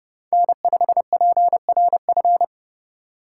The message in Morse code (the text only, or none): N5PRF